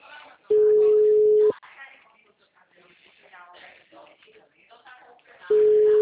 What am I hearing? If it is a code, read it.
Ringback tone (Europe, 425 Hz), with speech.